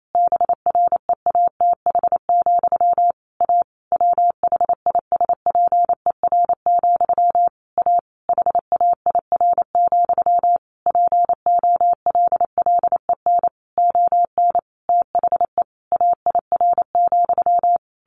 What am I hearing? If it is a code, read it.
Morse code, heard as a keyed tone, about 28 wpm, 700 Hz. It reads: BREATH, A WHISPER, A HAIR, POLLEN ON THE AIR,
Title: A Sound Of Thunder 28wpm